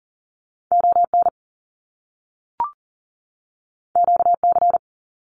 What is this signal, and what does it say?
Morse code, heard as a keyed tone, about 39 wpm, 700 Hz. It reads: ON QC
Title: Canadian Provinces and Territories 40wpm